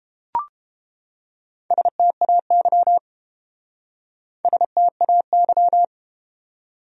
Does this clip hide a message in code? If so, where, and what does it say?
Yes: Morse code from 1.7 to 5.8 s, heard as a keyed tone, about 31 wpm, 700 Hz. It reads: STAY STAY